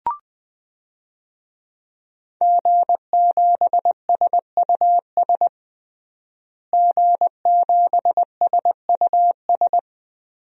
Morse code, heard as a keyed tone, about 20 wpm, 700 Hz. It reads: G7SUS G7SUS
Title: DXCC-100 Call Signs - 5 Characters - Repeated 3x - 20wpm